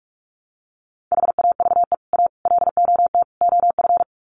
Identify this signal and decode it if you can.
Morse code, heard as a keyed tone, about 44 wpm, 700 Hz. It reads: HAVE A LOT OF
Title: Sets of 4 Words 45wpm